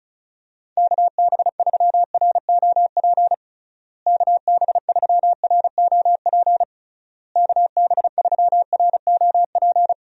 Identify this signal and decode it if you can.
Morse code, heard as a keyed tone, about 34 wpm, 700 Hz. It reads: KB3ROP KB3ROP KB3ROP
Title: USA Call Signs - 6 Characters - Repeated 3x - 35wpm